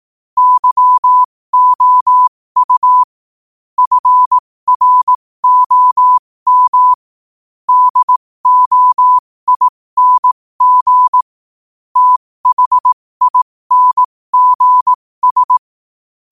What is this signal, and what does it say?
Morse code, heard as a keyed tone, about 18 wpm, spaced as for 15 wpm, 1000 Hz. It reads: YOU FROM DOING THINGS
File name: qotdfiles / 15wpm